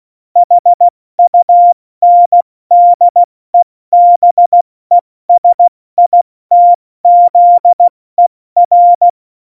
Morse code, heard as a keyed tone, about 16 wpm, 700 Hz. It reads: HUNDEBESITZER
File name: Morsekurs / Witze / Witz 25. Dezember, 70 F80